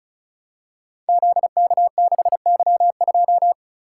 Morse code, heard as a keyed tone, about 35 wpm, 700 Hz. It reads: ZK6Y2
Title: Code Groups - 5 Letter:Numbers - 35wpm